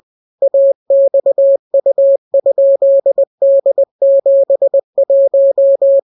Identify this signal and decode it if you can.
Morse code, heard as a keyed tone, about 20 wpm, 550 Hz. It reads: AXU?D71